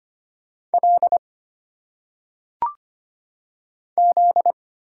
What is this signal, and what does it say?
Morse code, heard as a keyed tone, about 25 wpm, 700 Hz. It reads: L Z
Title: Single Letters - Phonetic Pronunciation 25wpm